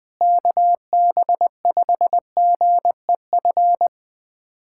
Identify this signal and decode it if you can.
Morse code, heard as a keyed tone, about 20 wpm, 700 Hz. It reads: KB5GEF
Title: USA Callsign DE Callsign - 6 Characters - 20wpm